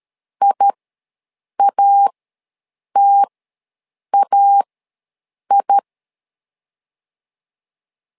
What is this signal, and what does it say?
Morse code, heard as a keyed tone, about 13 wpm, spaced as for 7 wpm, 800 Hz. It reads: IATAI
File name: LEZIONI / 2 UHDN / 2 U ws 12wpm 1655